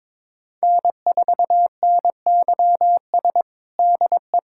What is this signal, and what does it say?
Morse code, heard as a keyed tone, about 21 wpm, 700 Hz. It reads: N4NYS DE
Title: USA Callsign DE Callsign - 5 Characters - 22wpm